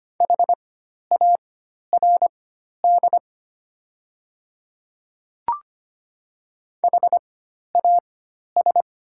Morse code, heard as a keyed tone, about 25 wpm, spaced as for 12 wpm, 690 Hz. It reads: HARD HAS
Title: 4x Character Spacing - Top 300 Words - 25wpm